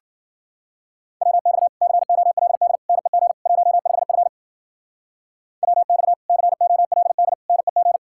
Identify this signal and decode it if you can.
Morse code, heard as a keyed tone, about 51 wpm, 700 Hz. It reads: WX COLD NEG 15F WX COLD NEG